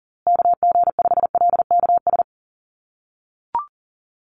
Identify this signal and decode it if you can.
Morse code, heard as a keyed tone, about 40 wpm, 700 Hz. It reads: KG5LKS
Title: USA Call Signs Repeated 3x 40wpm